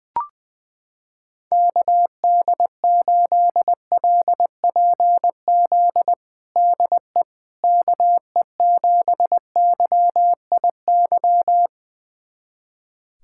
Morse code, heard as a keyed tone, about 21 wpm, 700 Hz. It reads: KD8LPZ DE KE7YIY
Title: USA Callsign DE Callsign - 6 Characters - 20wpm